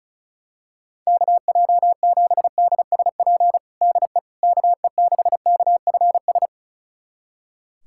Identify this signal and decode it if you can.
Morse code, heard as a keyed tone, about 36 wpm, 700 Hz. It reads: KJ7DSP DE KE6KFS
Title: USA Callsign DE Callsign - 6 Characters - 35wpm